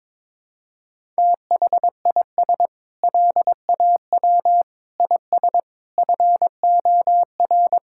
Morse code, heard as a keyed tone, about 22 wpm, 700 Hz. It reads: THIS LAW IS FOR
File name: practice / 22wpm / Sentences with 4-Word Phrases - 22wpm